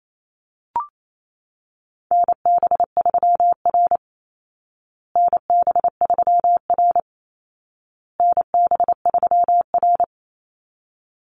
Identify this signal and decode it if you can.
Morse code, heard as a keyed tone, about 28 wpm, 700 Hz. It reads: NB3R NB3R NB3R
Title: Call Signs Repeated 2x - 28wpm